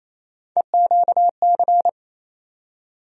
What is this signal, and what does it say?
Morse code, heard as a keyed tone, about 28 wpm, 700 Hz. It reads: EQC
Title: USA Call Sign Postfixes - 3 Letters 28wpm